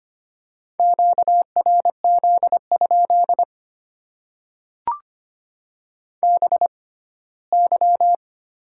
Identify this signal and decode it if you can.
Morse code, heard as a keyed tone, about 25 wpm, spaced as for 17 wpm, 690 Hz. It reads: QRZ? BY
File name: Top 90 QSO Elements - 6x Character Spacing - 25wpm